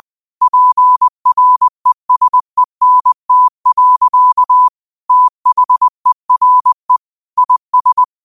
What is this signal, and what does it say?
Morse code, heard as a keyed tone, about 20 wpm, 1000 Hz. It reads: PRESENT. THERE IS
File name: qotdfiles / 20wpm